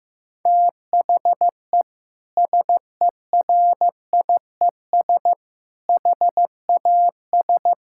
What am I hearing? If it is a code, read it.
Morse code, heard as a keyed tone, about 16 wpm, 700 Hz. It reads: THE SERIES HAS
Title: Sentences from Top 1000 Words - Review Part 5 - 15wpm